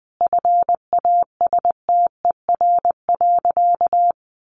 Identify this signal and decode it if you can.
Morse code, heard as a keyed tone, about 20 wpm, 700 Hz. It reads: FASTER.
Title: Additional Easy Phrases 20wpm